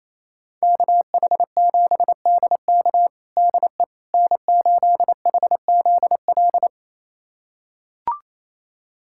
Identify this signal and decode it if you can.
Morse code, heard as a keyed tone, about 28 wpm, 700 Hz. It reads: KH7DK DE N8HZL
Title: USA Callsign DE Callsign - 5 Characters - 28wpm